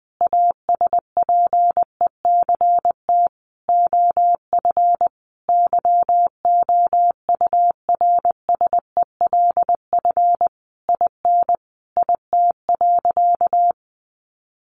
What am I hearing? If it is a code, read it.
Morse code, heard as a keyed tone, about 20 wpm, 700 Hz. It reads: ASPECT OF YOURSELF IN IT.
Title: Sentences from Top 1300 Words 20wpm